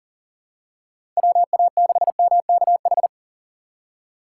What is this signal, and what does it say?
Morse code, heard as a keyed tone, about 40 wpm, 690 Hz. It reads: WA6MKH
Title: USA Call Signs - 6 Characters - 2x Character Spacing - 40wpm